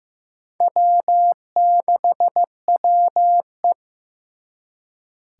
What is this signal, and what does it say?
Morse code, heard as a keyed tone, about 18 wpm, 700 Hz. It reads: W6WE
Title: Speed Racing - USA Call Signs 15wpm